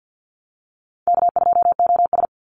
Morse code, heard as a keyed tone, about 50 wpm, 700 Hz. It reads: K2OS
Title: Easy Call Signs Repeated 2x - 50wpm